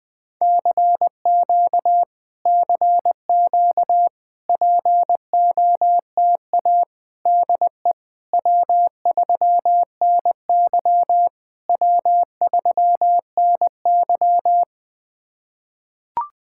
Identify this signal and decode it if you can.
Morse code, heard as a keyed tone, about 20 wpm, 690 Hz. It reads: CQ CQ POTA DE W3NY W3NY
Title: POTA QSO Practice - 20wpm